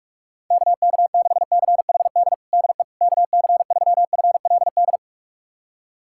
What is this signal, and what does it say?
Morse code, heard as a keyed tone, about 46 wpm, 690 Hz. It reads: KK6CHD DE KC3FLD
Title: USA Callsign DE Callsign - 6 Characters - 45wpm